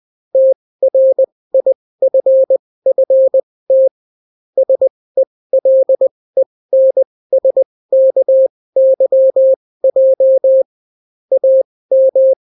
Morse code, heard as a keyed tone, about 20 wpm, spaced as for 16 wpm, 530 Hz. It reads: TRIFFT SELENSKYJ AM